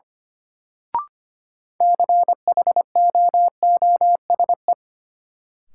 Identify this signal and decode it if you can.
Morse code, heard as a keyed tone, about 25 wpm, 700 Hz. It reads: CHOOSE
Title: Sentences from Top 700 Words - New Word Drill - 25wpm